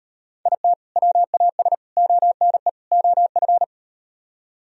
Morse code, heard as a keyed tone, about 38 wpm, 700 Hz. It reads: IT WAS ONE OF